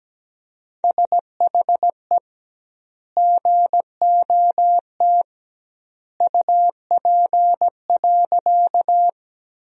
Morse code, heard as a keyed tone, about 17 wpm, 700 Hz. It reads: SHE GOT UP.
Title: 2x Word-Spacing - Sentences from Top 300 Words - 17wpm